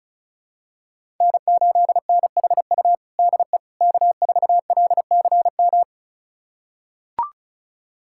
Morse code, heard as a keyed tone, about 36 wpm, 700 Hz. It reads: N8NHU DE K4LCM
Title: USA Callsign DE Callsign - 5 Characters - 35wpm